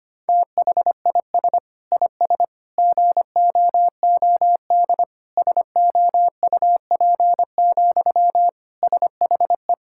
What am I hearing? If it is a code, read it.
Morse code, heard as a keyed tone, about 25 wpm, 700 Hz. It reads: THIS IS GOOD SOUP, SHE
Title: Sentences from Top 1000 Words - Review Part 6 - 25wpm